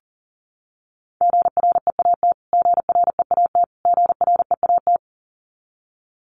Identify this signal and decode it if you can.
Morse code, heard as a keyed tone, about 40 wpm, 700 Hz. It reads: GREAT GREAT GREAT
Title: Top 500 Words Repeated 40wpm